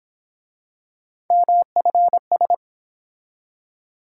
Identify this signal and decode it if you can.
Morse code, heard as a keyed tone, about 28 wpm, 700 Hz. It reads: MFS
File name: Speed-Racing - Three Letter-Number Combinations - 22wpm